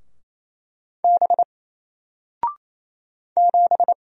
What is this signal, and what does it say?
Morse code, heard as a keyed tone, about 27 wpm, 700 Hz. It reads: B 7
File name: Single-Letter-Number-28wpm